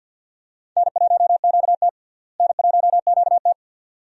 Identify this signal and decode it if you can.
Morse code, heard as a keyed tone, about 48 wpm, 700 Hz. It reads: N1QT N1QT
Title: USA Call Signs - 4 Characters - Repeated 2x - 50wpm